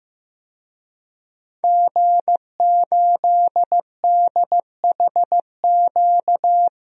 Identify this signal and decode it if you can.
Morse code, heard as a keyed tone, about 15 wpm, 700 Hz. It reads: G8DHQ
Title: DXCC-100 Call Signs - 5 Characters - 15wpm